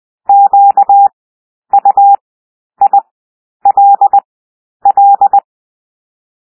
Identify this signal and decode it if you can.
Morse code, heard as a keyed tone, about 20 wpm, 800 Hz. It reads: Q U I L L
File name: cwmp3 / W1 SOUND WORDS 20 wpm